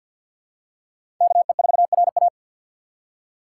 Morse code, heard as a keyed tone, about 53 wpm, 700 Hz. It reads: KE4RA
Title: USA Call Signs - 2x Character Spacing - 50wpm